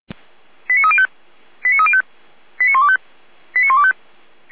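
Five-tone selective call (ZVEI), heard as quick runs of five tones, tones of about 70 ms. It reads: 89305 89305 89125 89125
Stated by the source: ZVEI five-tone selective call 89277 received by SDR, from the feuerwehr-orlamuende (Orlamünde fire brigade) audio server.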